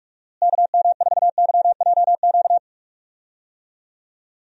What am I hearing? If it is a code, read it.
Morse code, heard as a keyed tone, about 47 wpm, 700 Hz. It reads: KM4YJQ
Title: USA Call Signs - 6 Characters - Repeated 3x - 45wpm